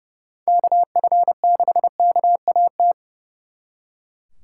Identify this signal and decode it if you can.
Morse code, heard as a keyed tone, about 36 wpm, 700 Hz. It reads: KF6KAT
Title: Speed Racing - USA 2x3 Call Signs 30wpm